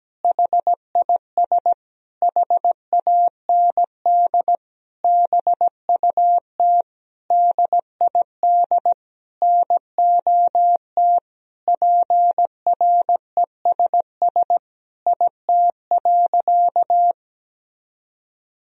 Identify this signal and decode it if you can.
Morse code, heard as a keyed tone, about 17 wpm, 700 Hz. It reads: HIS HAND BUT DID NOT PRESS IT.
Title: Sentences from Top 1000 Words - Review Part 1 - 17wpm